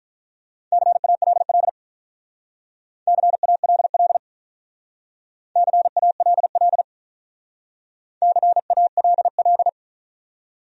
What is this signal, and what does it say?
Morse code, heard as a keyed tone, about 45 wpm, 700 Hz. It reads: CALL CALL CALL CALL